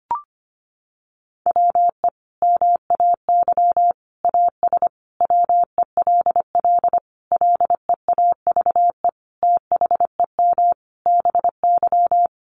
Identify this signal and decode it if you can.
Morse code, heard as a keyed tone, about 25 wpm, 700 Hz. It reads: WE MAY AS WELL LEAVE THEM BY
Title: Sentences from Top 800 Words 25wpm